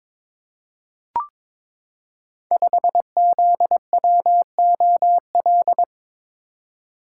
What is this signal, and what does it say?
Morse code, heard as a keyed tone, about 22 wpm, 690 Hz. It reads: 5ZWOL
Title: Code Groups - 5 Letter:Numbers - 22wpm